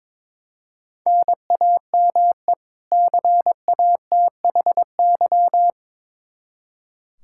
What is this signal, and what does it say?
Morse code, heard as a keyed tone, about 22 wpm, 700 Hz. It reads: NAME CATHY
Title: Practice QSO - Exchange 1 - Beginning - 22wpm